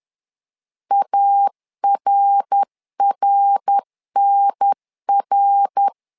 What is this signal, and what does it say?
Morse code, heard as a keyed tone, about 10 wpm, 800 Hz. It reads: ARRNR
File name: LEZIONI / 3 MWRO / 3 MWR 10wpm 1952